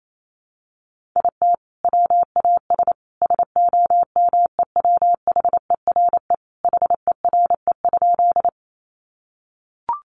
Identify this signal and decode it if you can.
Morse code, heard as a keyed tone, about 28 wpm, 700 Hz. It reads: IT WAS SOMEWHERE HERE?
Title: Sentences from Top 1400 Words 28wpm